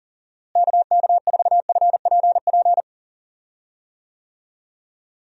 Morse code, heard as a keyed tone, about 39 wpm, 700 Hz. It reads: KK4FPP
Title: USA Call Signs - 6 Characters - Repeated 3x - 40wpm